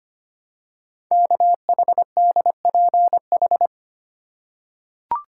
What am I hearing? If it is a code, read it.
Morse code, heard as a keyed tone, about 25 wpm, 700 Hz. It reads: KHDPH
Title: Code Groups - 5 Letters - 25wpm